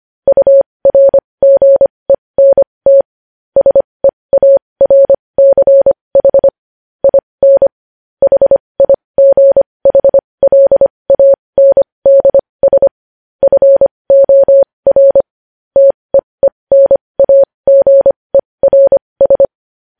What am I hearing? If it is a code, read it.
Morse code, heard as a keyed tone, about 25 wpm, spaced as for 20 wpm, 550 Hz. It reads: URGENT SEARCH IN HIGHLANDS FOR TEENAGERS
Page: news-headlines-in-morse-code-at-20-wpm